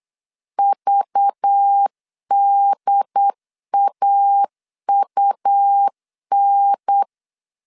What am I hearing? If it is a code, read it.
Morse code, heard as a keyed tone, about 8 wpm, 800 Hz. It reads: VDAUN
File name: 8wpm